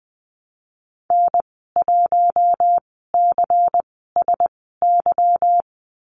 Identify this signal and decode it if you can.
Morse code, heard as a keyed tone, about 20 wpm, spaced as for 15 wpm, 700 Hz. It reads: N1CSY
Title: 179 Lesson 29g - Slash - Call Signs 20wpm